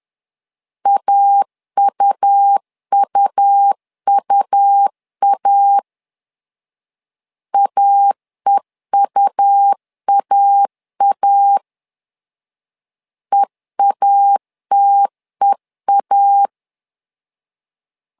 Morse code, heard as a keyed tone, about 10 wpm, 800 Hz. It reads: AUUUA AEUAA EATEA
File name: LEZIONI / 2 UHDN / 2 U 10wpm 1622